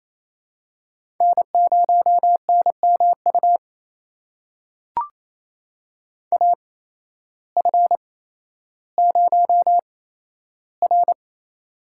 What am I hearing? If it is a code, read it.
Morse code, heard as a keyed tone, about 28 wpm, spaced as for 13 wpm, 700 Hz. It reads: N0NMU AF0R
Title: USA Call Signs - 8x Character Spacing - 28wpm